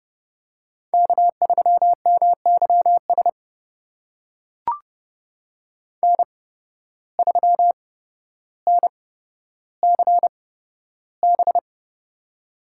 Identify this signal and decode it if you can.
Morse code, heard as a keyed tone, about 30 wpm, spaced as for 13 wpm, 700 Hz. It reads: K3MYS N3NCB
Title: USA Call Signs - 5 Characters - 8x Character Spacing - 30wpm